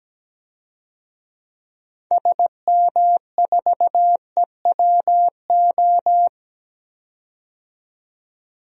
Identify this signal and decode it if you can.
Morse code, heard as a keyed tone, about 17 wpm, 700 Hz. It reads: SM4EWO
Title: DXCC-100 Call Signs - 6 Characters - 17wpm